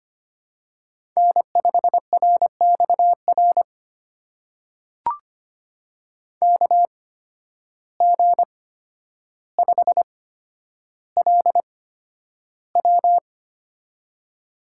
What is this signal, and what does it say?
Morse code, heard as a keyed tone, about 25 wpm, spaced as for 11 wpm, 700 Hz. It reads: N5RXR KG5LW
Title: USA Call Signs - 5 Characters - 8x Character Spacing - 25wpm